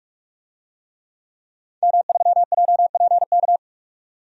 Morse code, heard as a keyed tone, about 46 wpm, 700 Hz. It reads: M3JPK
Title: DXCC-100 Call Signs - 5 Characters - 45wpm